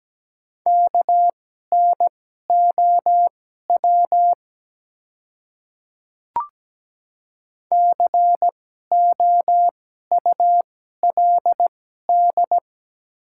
Morse code, heard as a keyed tone, about 17 wpm, spaced as for 12 wpm, 700 Hz. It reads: KNOW COULD
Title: 2x Character Spacing - Top 100 Words - 17wpm